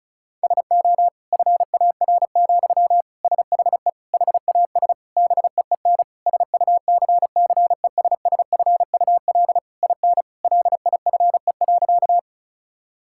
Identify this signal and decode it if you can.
Morse code, heard as a keyed tone, about 34 wpm, 700 Hz. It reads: SO FAR, SHE HAS BEEN SUCCESSFUL IN LIFE.
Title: Sentences from Top 1000 Words - Review Part 6 - 35wpm